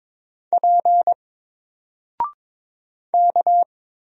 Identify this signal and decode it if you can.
Morse code, heard as a keyed tone, about 22 wpm, 700 Hz. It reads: P K
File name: Single-Letter-Number-22wpm